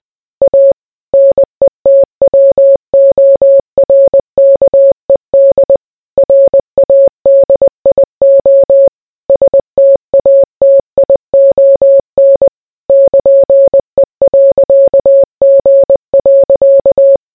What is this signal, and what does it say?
Morse code, heard as a keyed tone, about 20 wpm, 550 Hz. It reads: A NETWORKED RADIO STATION <KN>E.G.